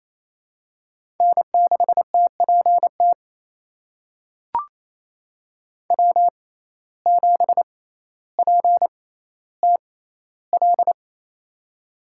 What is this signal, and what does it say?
Morse code, heard as a keyed tone, about 28 wpm, spaced as for 14 wpm, 700 Hz. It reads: N6TPT W7PTL
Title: USA Call Signs - 6x Character Spacing - 28wpm